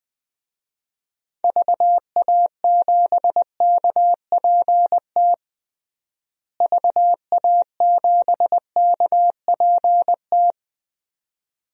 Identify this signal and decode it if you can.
Morse code, heard as a keyed tone, about 20 wpm, 700 Hz. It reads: VA7KPT VA7KPT
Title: DXCC-100 Call Signs - 6 Characters - Repeated 2x - 20wpm